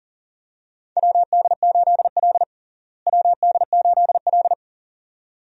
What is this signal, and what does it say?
Morse code, heard as a keyed tone, about 40 wpm, 700 Hz. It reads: WD8L WD8L
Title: USA Call Signs - 4 Characters - Repeated 2x - 40wpm